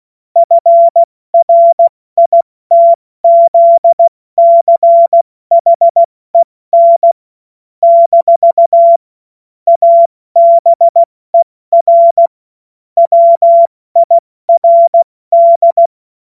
Morse code, heard as a keyed tone, about 16 wpm, 680 Hz. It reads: FRITZCHEN - ABER WIRD
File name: Morsekurs / Witze / Witz 03. August 2021, 70 F80